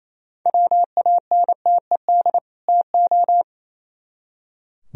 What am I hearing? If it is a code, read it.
Morse code, heard as a keyed tone, about 36 wpm, 700 Hz. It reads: WANTED TO